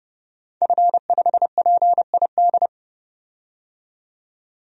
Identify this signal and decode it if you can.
Morse code, heard as a keyed tone, about 29 wpm, 700 Hz. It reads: F5PID
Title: DXCC-100 Call Signs - 5 Characters - 30wpm